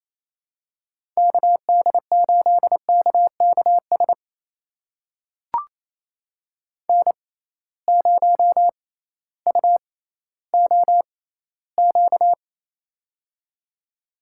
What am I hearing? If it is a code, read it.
Morse code, heard as a keyed tone, about 28 wpm, spaced as for 15 wpm, 700 Hz. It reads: KD8KKS N0UOQ
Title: USA Call Signs - 6x Character Spacing - 28wpm